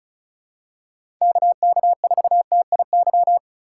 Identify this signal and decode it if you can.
Morse code, heard as a keyed tone, about 36 wpm, 700 Hz. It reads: KK4TIY
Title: USA Call Signs - 4x Character Spacing - 35wpm